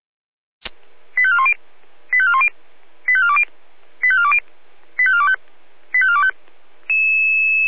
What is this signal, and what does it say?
Five-tone selective call (ZVEI), heard as quick runs of five tones, tones of about 70 ms. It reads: 85410 85410 85420 85420 85425 85425